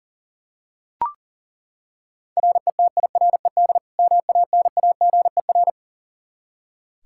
Morse code, heard as a keyed tone, about 40 wpm, 700 Hz. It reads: RETIRED MANAGER
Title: K1USN SST QSO Practice 40wpm